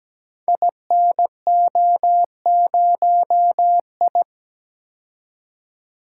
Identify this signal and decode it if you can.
Morse code, heard as a keyed tone, about 17 wpm, 700 Hz. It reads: INO0I
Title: Code Groups - 5 Letter:Numbers - 17wpm